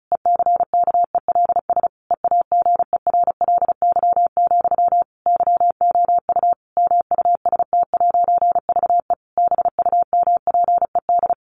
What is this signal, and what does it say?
Morse code, heard as a keyed tone, about 34 wpm, 700 Hz. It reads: ECKELS EAGERLY, YOU MUST'VE BUMPED
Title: A Sound Of Thunder 35wpm